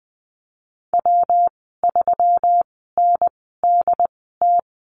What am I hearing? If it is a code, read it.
Morse code, heard as a keyed tone, about 19 wpm, spaced as for 14 wpm, 700 Hz. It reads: W3NDT
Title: 154 Lesson 25g - Letter p - Call Signs 20wpm